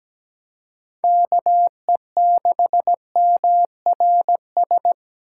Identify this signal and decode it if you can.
Morse code, heard as a keyed tone, about 17 wpm, 700 Hz. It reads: KE6MRS
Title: USA Call Signs - 6 Characters - 4x Character Spacing - 17wpm